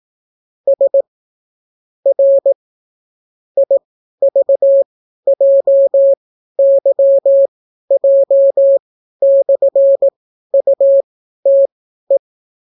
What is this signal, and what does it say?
Morse code, heard as a keyed tone, about 18 wpm, spaced as for 12 wpm, 550 Hz. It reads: S R IVJYJ/UTE